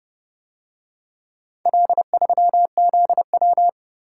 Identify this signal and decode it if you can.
Morse code, heard as a keyed tone, about 29 wpm, 700 Hz. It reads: L3ZW